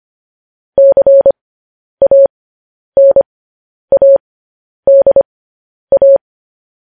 Morse code, heard as a keyed tone, about 25 wpm, spaced as for 10 wpm, 550 Hz. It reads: CANADA
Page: news-headlines-in-morse-code-at-10-wpm